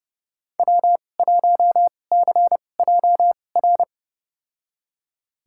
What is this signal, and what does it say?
Morse code, heard as a keyed tone, about 30 wpm, spaced as for 22 wpm, 700 Hz. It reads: W1CJR
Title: USA Call Signs - 5 Characters - 2x Character Spacing - 30wpm